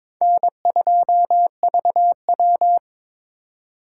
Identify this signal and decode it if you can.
Morse code, heard as a keyed tone, about 26 wpm, 700 Hz. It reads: N2VW
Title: Speed Racing - USA 1x2 Call Signs 22wpm